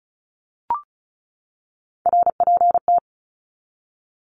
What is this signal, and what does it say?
Morse code, heard as a keyed tone, about 34 wpm, 700 Hz. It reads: RPT
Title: Abbreviations 35wpm